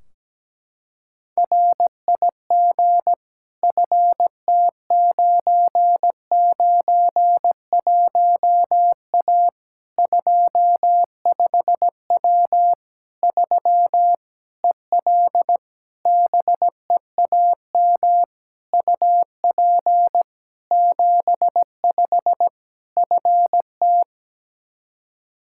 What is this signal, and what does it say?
Morse code, heard as a keyed tone, about 17 wpm, 700 Hz. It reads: RIG FT991A 25W 3 EL BEAM UP 75 FT
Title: K1USN SST QSO Practice 17wpm